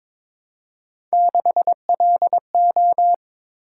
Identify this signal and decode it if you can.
Morse code, heard as a keyed tone, about 21 wpm, 700 Hz. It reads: 6LO
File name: practice / 22wpm / Three Letter-Number Combinations - 22wpm